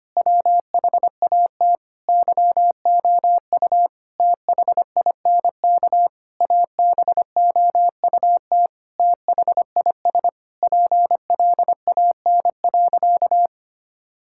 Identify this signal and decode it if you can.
Morse code, heard as a keyed tone, about 25 wpm, 700 Hz. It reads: WHAT YOU THINK ABOUT THIS PLAN.
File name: practice / 25wpm / Sentences with 9-Word Phrases - 25wpm